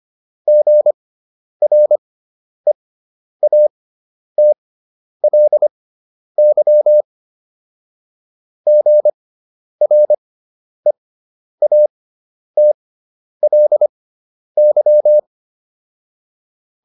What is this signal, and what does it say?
Morse code, heard as a keyed tone, about 25 wpm, spaced as for 10 wpm, 600 Hz. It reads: GREATLY GREATLY
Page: fundamental-practice-files